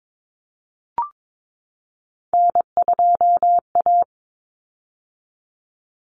Morse code, heard as a keyed tone, about 22 wpm, 700 Hz. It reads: N2A
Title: Common Cut Numbers - 3-Digit 22wpm